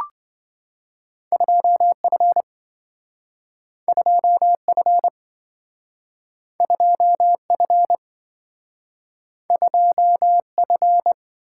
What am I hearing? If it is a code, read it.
Morse code, heard as a keyed tone, about 27 wpm, 700 Hz. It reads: 2F 2F 2F 2F